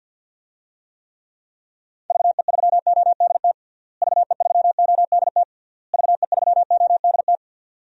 Morse code, heard as a keyed tone, about 52 wpm, 700 Hz. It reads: VE3ODT VE3ODT VE3ODT
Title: DXCC-100 Call Signs - 6 Characters - Repeated 3x - 50wpm